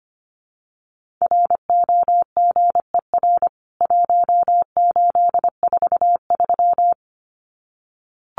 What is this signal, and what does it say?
Morse code, heard as a keyed tone, about 25 wpm, 700 Hz. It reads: ROGER 1843
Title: CWT Exchange 25wpm